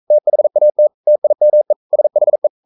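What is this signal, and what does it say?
Morse code, heard as a keyed tone, about 42 wpm, 600 Hz. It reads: THAT TIME SHE
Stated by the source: Morse code sending of an English novel at 42 wpm.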